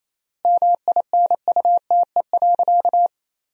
Morse code, heard as a keyed tone, about 28 wpm, 700 Hz. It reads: MINUTE.
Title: 3x Word-Spacing - Sentences from Top 500 Words - 28wpm